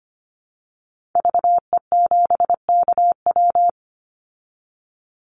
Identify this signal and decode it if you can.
Morse code, heard as a keyed tone, about 25 wpm, 700 Hz. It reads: VE7KW
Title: Call Signs Repeated 2x - 25wpm